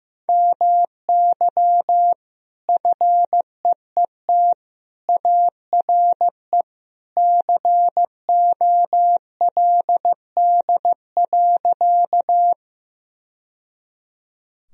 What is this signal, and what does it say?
Morse code, heard as a keyed tone, about 15 wpm, 700 Hz. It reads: MY FEET ARE COLD.
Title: Sentences from Top 1000 Words - Review Part 1 - 15wpm